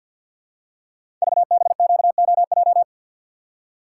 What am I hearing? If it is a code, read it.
Morse code, heard as a keyed tone, about 52 wpm, 700 Hz. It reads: VBQOJ
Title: Code Groups - 5 Letters - 50wpm